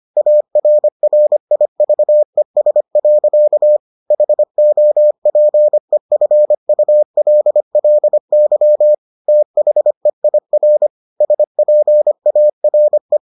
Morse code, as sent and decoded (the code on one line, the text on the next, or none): .- .-. .-. .. ...- . ... .-.-.- / .... --- .--. ..-. ..- .-.. .-.. -.-- / - .... .. .-. / ... .--. .- .-. .
ARRIVES. HOPEFULLY THEIR SPARE